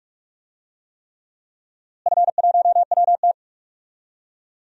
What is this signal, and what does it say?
Morse code, heard as a keyed tone, about 42 wpm, 700 Hz. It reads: F1WT
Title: DXCC-100 Call Signs - 4 Characters - Repeated 3x - 45wpm